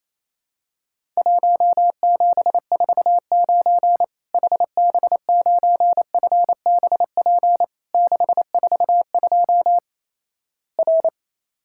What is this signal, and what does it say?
Morse code, heard as a keyed tone, about 28 wpm, 700 Hz. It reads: 1749 HB9FBP 642 R
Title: Worked All Europe DX Contest - QTC Exchange - 28wpm